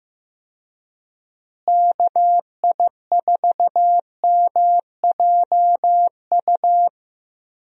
Morse code, heard as a keyed tone, about 15 wpm, 700 Hz. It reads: KI4MJU
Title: USA Call Signs - 6 Characters - Repeated 3x - 15wpm